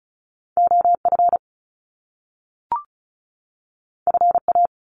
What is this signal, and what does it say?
Morse code, heard as a keyed tone, about 36 wpm, 700 Hz. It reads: OF FA
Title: Top Two Letter Combinations in Words 35wpm